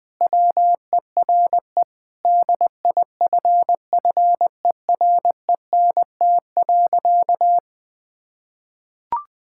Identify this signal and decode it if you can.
Morse code, heard as a keyed tone, about 21 wpm, 700 Hz. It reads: WERE DIFFERENT.
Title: Sentences with 6-Word Phrases - Spoken Beginning - 20wpm